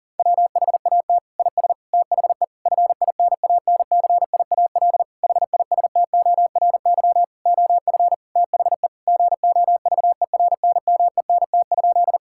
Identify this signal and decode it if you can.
Morse code, heard as a keyed tone, about 40 wpm, 700 Hz. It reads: WHAT IS THE FINANCIAL HISTORY OF THE GOVERNMENT?
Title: Sentences from Top 1000 Words - Review Part 5 - 40wpm